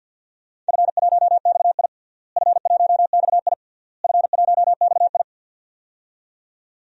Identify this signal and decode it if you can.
Morse code, heard as a keyed tone, about 51 wpm, 700 Hz. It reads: F1XI F1XI F1XI
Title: DXCC-100 Call Signs - 4 Characters - Repeated 3x - 50wpm